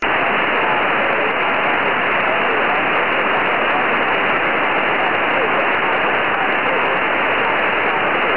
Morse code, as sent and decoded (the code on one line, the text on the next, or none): none